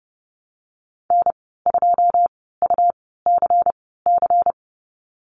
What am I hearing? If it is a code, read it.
Morse code, heard as a keyed tone, about 30 wpm, spaced as for 17 wpm, 700 Hz. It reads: N2UCC
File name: Lesson 19g - Letter w - Call Signs 30wpm